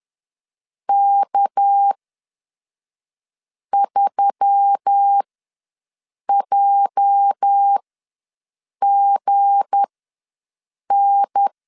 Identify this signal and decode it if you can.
Morse code, heard as a keyed tone, about 10 wpm, spaced as for 5 wpm, 800 Hz. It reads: K3JGN
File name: LEZIONI / 9 28 ws 10wpm 4124